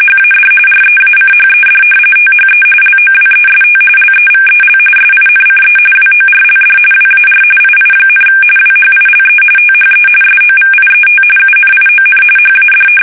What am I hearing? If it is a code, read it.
RTTY, about 45.45 baud. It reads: HAD REACHED THE LOWEST LEVEL SINCE WORLD WAR II, WHILE THE SOVIET UNION HAD
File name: fsk850 / battleshipnj